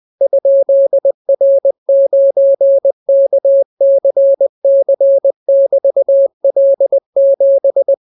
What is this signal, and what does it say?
Morse code, heard as a keyed tone, about 20 wpm, 550 Hz. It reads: ?R9KCC<BT>L7